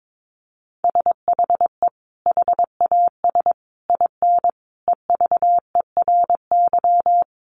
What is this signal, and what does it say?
Morse code, heard as a keyed tone, about 22 wpm, 700 Hz. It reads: SHE HAS IN EVERY
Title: Sentences from Top 1300 Words 22wpm